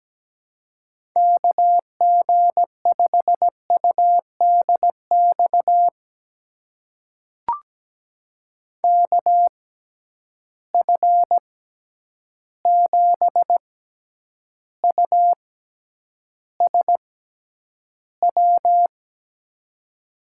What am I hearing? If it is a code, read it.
Morse code, heard as a keyed tone, about 17 wpm, spaced as for 9 wpm, 700 Hz. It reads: KG5UDX KF7USW
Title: USA Call Signs - 6 Characters - 6x Character Spacing - 17wpm